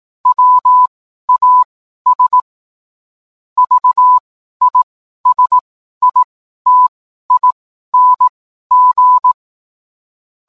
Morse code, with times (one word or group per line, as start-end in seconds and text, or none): 0.3-2.4 WAS
3.6-9.3 VISITING